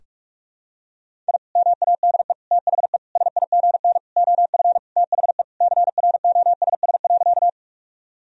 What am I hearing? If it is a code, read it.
Morse code, heard as a keyed tone, about 46 wpm, 700 Hz. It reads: I MADE THE SIGN OF THE CROSS.
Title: Sentences from Top 1000 Words - Review Part 1 - 45wpm